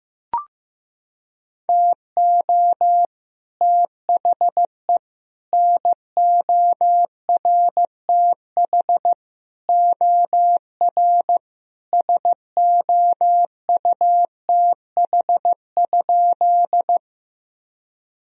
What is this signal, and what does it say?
Morse code, heard as a keyed tone, about 15 wpm, 700 Hz. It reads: TO THE NORTH OR SOUTH?
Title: Sentences from Top 1000 Words - Review Part 2 - 15wpm